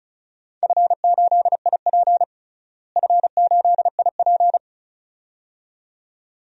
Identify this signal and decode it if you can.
Morse code, heard as a keyed tone, about 36 wpm, 690 Hz. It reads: F8IP F8IP
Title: DXCC-100 Call Signs - 4 Characters - Repeated 2x - 35wpm